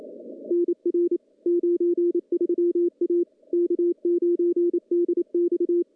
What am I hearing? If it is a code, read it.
Morse code, heard as a keyed tone, about 27 wpm, 350 Hz. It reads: NR 93 A K9DX
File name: contests / CW-SS-k5tr- -0100